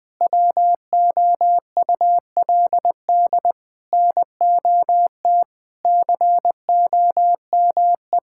Morse code, heard as a keyed tone, about 20 wpm, 700 Hz. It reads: WOULD NOT COME
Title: Sentences from Top 1600 Words 20wpm